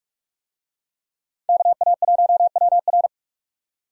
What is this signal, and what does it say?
Morse code, heard as a keyed tone, about 44 wpm, 700 Hz. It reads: KA1WR